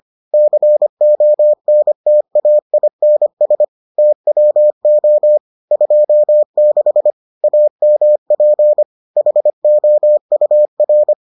Morse code, heard as a keyed tone, about 25 wpm, 600 Hz. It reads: CONTAINS TWO 26 AMP HOUR